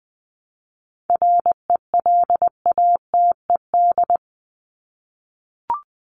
Morse code, heard as a keyed tone, about 20 wpm, 700 Hz. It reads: RELATED
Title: Top 751-1000 Words 20wpm